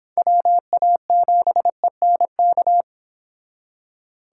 Morse code, heard as a keyed tone, about 30 wpm, 700 Hz. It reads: WA7ENK